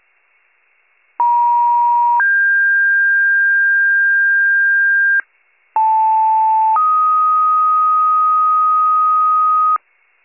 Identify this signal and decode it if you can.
Two-tone page, heard as a one-second tone, then a three-second tone, about 950 then 1600 Hz, about 880 then 1250 Hz. It is heard from 1.2 to 9.8 s.